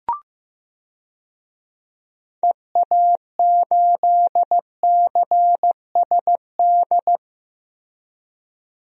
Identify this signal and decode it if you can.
Morse code, heard as a keyed tone, about 15 wpm, 700 Hz. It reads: EA8CSD
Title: DXCC-100 Call Signs - 6 Characters - Repeated 3x - 15wpm